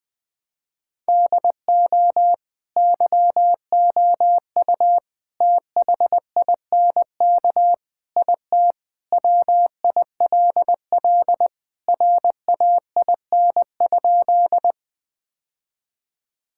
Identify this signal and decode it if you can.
Morse code, heard as a keyed tone, about 20 wpm, 700 Hz. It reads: DO YOU THINK IT WILL RAIN?
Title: Sentences with 4-Word Phrases - Spoken Beginning - 20wpm